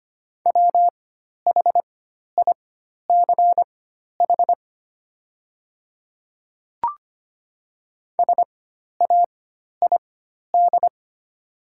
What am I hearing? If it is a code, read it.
Morse code, heard as a keyed tone, about 25 wpm, spaced as for 12 wpm, 700 Hz. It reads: WHICH SAID
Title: 4x Character Spacing - Top 100 Words - 25wpm